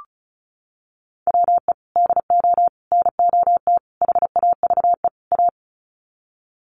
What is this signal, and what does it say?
Morse code, heard as a keyed tone, about 34 wpm, 700 Hz. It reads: WE DO NOT HAVE A
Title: Sets of 5 Words 35wpm